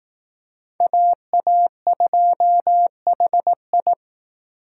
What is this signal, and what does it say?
Morse code, heard as a keyed tone, about 18 wpm, 700 Hz. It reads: AA2HI
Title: Speed Racing - USA 2x2 Call Signs 15wpm